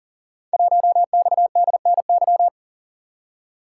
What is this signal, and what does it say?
Morse code, heard as a keyed tone, about 42 wpm, 700 Hz. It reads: 1XDNY
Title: Code Groups - 5 Letter:Numbers - 40wpm